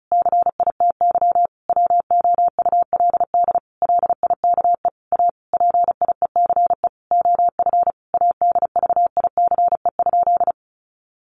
Morse code, heard as a keyed tone, about 34 wpm, 700 Hz. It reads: CITY WOULD LIKE A PIECE OF ADVICE?
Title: Sentences from Top 1100 Words 35wpm